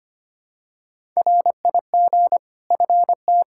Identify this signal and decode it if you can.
Morse code, heard as a keyed tone, about 25 wpm, 700 Hz. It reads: RIG FT
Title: Practice QSO - Exchange 2 - Info - 25wpm